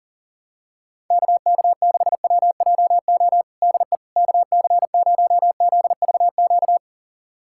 Morse code, heard as a keyed tone, about 40 wpm, 700 Hz. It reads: KK6WJO DE KC0ZVQ
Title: USA Callsign DE Callsign 40wpm